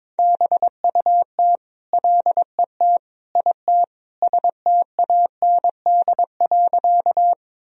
Morse code, heard as a keyed tone, about 22 wpm, 700 Hz. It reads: BUT LET IT STAND.
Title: Sentences from Top 1000 Words - Review Part 1 - 22wpm